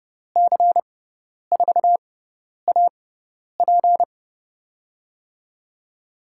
Morse code, heard as a keyed tone, about 30 wpm, spaced as for 10 wpm, 700 Hz. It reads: C4AP
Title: USA Call Signs - 6x Character Spacing - 30wpm